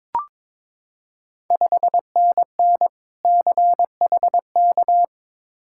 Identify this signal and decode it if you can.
Morse code, heard as a keyed tone, about 22 wpm, 700 Hz. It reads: 5NN CHK
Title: Ontario QSO Party Exchange 22wpm